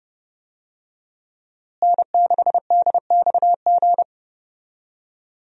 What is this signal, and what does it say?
Morse code, heard as a keyed tone, about 30 wpm, 700 Hz. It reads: N6DXG
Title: USA Call Signs - 5 Characters - Repeated 3x - 30wpm